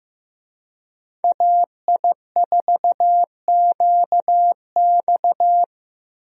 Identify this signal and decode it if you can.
Morse code, heard as a keyed tone, about 15 wpm, 700 Hz. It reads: AI4QX
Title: USA Call Signs - 5 Characters - Repeated 2x - 15wpm